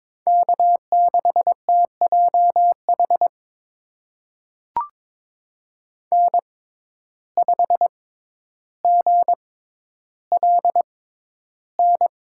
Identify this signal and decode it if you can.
Morse code, heard as a keyed tone, about 22 wpm, spaced as for 11 wpm, 700 Hz. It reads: K6TJH N5GLN
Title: USA Call Signs - 5 Characters - 6x Character Spacing - 22wpm